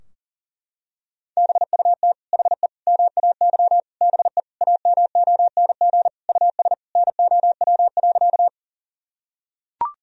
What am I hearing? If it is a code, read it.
Morse code, heard as a keyed tone, about 40 wpm, 700 Hz. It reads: BUT HE MAY BE AMONG US NOW.
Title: Sentences from Top 1000 Words - Review Part 3 - 40wpm